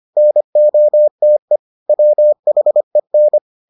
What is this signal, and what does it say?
Morse code, heard as a keyed tone, about 25 wpm, 600 Hz. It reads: NOTE WHEN